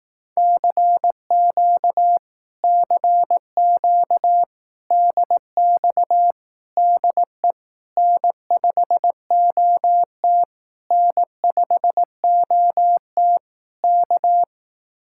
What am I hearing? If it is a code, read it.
Morse code, heard as a keyed tone, about 18 wpm, 700 Hz. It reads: CQ CQ DX DE N5OT N5OT K
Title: Speed-Racing Common QSO Phrases Encore - 15wpm